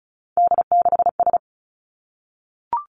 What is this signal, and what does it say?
Morse code, heard as a keyed tone, about 36 wpm, 700 Hz. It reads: D6S
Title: Three Challenging Letter-Number Combinations 35wpm